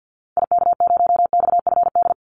Morse code, heard as a keyed tone, about 50 wpm, 700 Hz. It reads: IK0XFD
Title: Call Signs - 50wpm